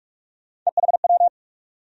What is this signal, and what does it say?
Morse code, heard as a keyed tone, about 46 wpm, 700 Hz. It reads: EHW
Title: USA Call Sign Postfixes - 3 Letters 45wpm